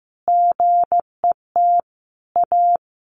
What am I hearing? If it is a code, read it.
Morse code, heard as a keyed tone, about 15 wpm, 700 Hz. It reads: GET A
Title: Sentences from Top 1400 Words 15wpm